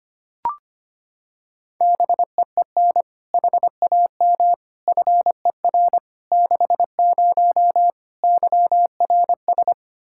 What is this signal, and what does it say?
Morse code, heard as a keyed tone, about 25 wpm, 700 Hz. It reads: BEEN HAM FER 60 YRS
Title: Practice QSO - Exchange 3 - More Info - 25wpm